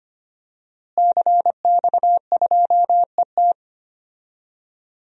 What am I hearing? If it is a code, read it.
Morse code, heard as a keyed tone, about 25 wpm, 700 Hz. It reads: CX2ET
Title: DXCC-100 Call Signs - 5 Characters - Repeated 3x - 25wpm